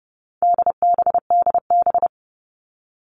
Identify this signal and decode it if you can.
Morse code, heard as a keyed tone, about 30 wpm, 700 Hz. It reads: DBDB
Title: All Cut Numbers - 4-Digit 30wpm